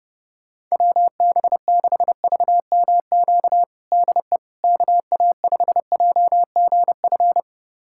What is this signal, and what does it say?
Morse code, heard as a keyed tone, about 32 wpm, 700 Hz. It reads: WB6VMQ DE KA5JGF
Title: USA Callsign DE Callsign - 6 Characters - 30wpm